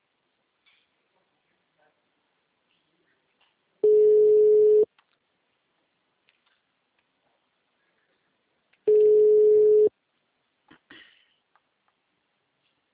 Ringback tone (Europe, 425 Hz).